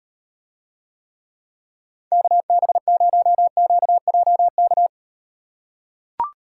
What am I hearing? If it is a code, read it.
Morse code, heard as a keyed tone, about 38 wpm, 700 Hz. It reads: KB0QJK